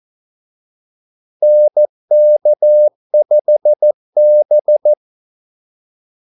Morse code, heard as a keyed tone, about 14 wpm, 600 Hz. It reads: NK5B
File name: Callsigns-14wpm